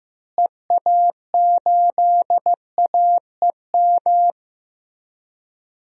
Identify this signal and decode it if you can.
Morse code, heard as a keyed tone, about 15 wpm, 700 Hz. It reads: EA8AEM
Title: DXCC-100 Call Signs - 6 Characters - Repeated 3x - 15wpm